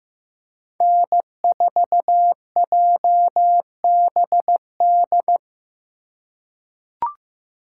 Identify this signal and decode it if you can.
Morse code, heard as a keyed tone, about 15 wpm, 700 Hz. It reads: N4JBD
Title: USA Call Signs - 5 Characters - 2x Character Spacing - 15wpm